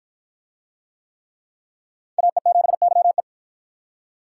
Morse code, heard as a keyed tone, about 53 wpm, 700 Hz. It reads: AE7YE